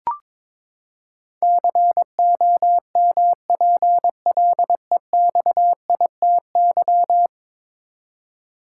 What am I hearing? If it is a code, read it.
Morse code, heard as a keyed tone, about 22 wpm, 700 Hz. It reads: COMPLEXITY
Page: morse-code-ninja-practice-22wpm